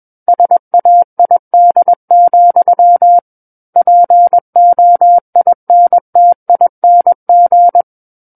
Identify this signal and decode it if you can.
Morse code, heard as a keyed tone, about 21 wpm, 700 Hz. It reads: SAID, POINTING
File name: Book One / 12-Chapter 12 - 21 WPM